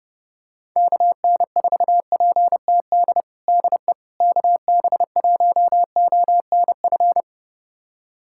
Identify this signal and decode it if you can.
Morse code, heard as a keyed tone, about 30 wpm, 700 Hz. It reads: KN4PTD DE KB1ONF
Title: USA Callsign DE Callsign - 6 Characters - 30wpm